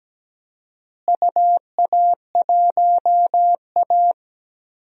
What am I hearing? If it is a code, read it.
Morse code, heard as a keyed tone, about 17 wpm, 700 Hz. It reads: UA1A
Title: DXCC-100 Call Signs - 4 Characters - Repeated 2x - 17wpm